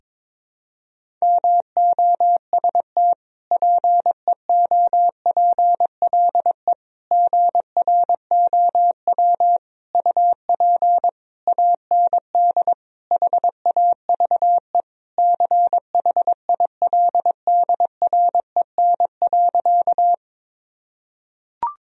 Morse code, heard as a keyed tone, about 22 wpm, 700 Hz. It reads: MOST PEOPLE GROW UP AND HAVE CHILDREN.
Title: Sentences from Top 1000 Words - Review Part 1 - 22wpm